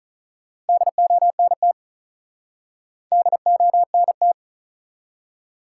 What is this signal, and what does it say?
Morse code, heard as a keyed tone, about 47 wpm, 700 Hz. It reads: DONT DONT